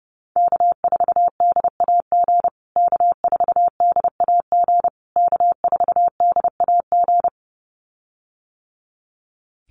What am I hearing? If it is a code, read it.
Morse code, heard as a keyed tone, about 28 wpm, 700 Hz. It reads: K4DAG K4DAG K4DAG
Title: USA Call Signs Repeated 3x 30wpm